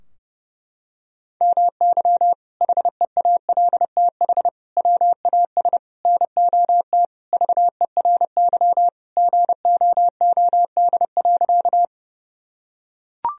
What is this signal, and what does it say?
Morse code, heard as a keyed tone, about 30 wpm, 700 Hz. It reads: MY HEALTH WAS NOT VERY GOOD.
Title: Sentences from Top 1000 Words - Review Part 3 - 30wpm